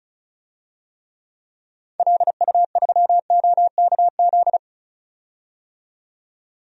Morse code, heard as a keyed tone, about 34 wpm, 700 Hz. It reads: LU3OKZ
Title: DXCC-100 Call Signs - 6 Characters - 35wpm